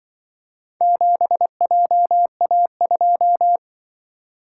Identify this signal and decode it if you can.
Morse code, heard as a keyed tone, about 25 wpm, 700 Hz. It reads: 7JA2
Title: Speed-Racing - Four Letter-Number Combinations - 20wpm